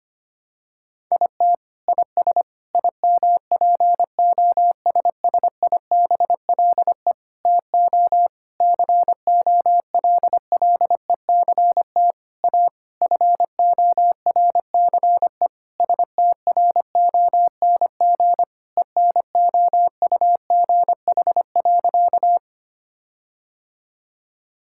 Morse code, heard as a keyed tone, about 25 wpm, 700 Hz. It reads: IT IS IMPOSSIBLE TO COLLECT A FORCE STRONG ENOUGH.
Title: Sentences from Top 1600 Words 25wpm